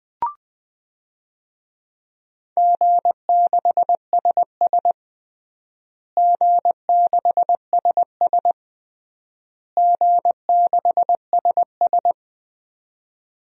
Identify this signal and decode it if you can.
Morse code, heard as a keyed tone, about 20 wpm, 700 Hz. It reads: G6SS G6SS G6SS
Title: DXCC-100 Call Signs - 4 Characters - Repeated 3x - 20wpm